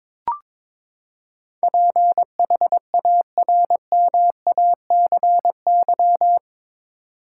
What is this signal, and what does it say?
Morse code, heard as a keyed tone, about 22 wpm, 700 Hz. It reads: PHARMACY
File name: practice / 22wpm / Eight Letter Words - 22wpm